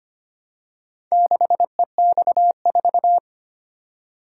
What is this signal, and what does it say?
Morse code, heard as a keyed tone, about 25 wpm, 700 Hz. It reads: 6EX4
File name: practice / 25wpm / Four Letter-Number Combinations - 25wpm